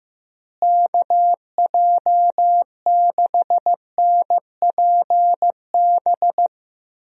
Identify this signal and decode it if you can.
Morse code, heard as a keyed tone, about 15 wpm, 700 Hz. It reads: KJ6NPB
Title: USA Call Signs - 6 Characters - Repeated 3x - 15wpm